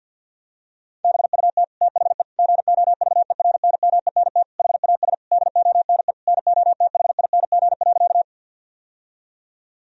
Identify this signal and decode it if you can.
Morse code, heard as a keyed tone, about 50 wpm, 700 Hz. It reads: BUT THE GOVERNMENT HAS DONE NOTHING.
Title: Sentences from Top 1000 Words - Review Part 3 - 50wpm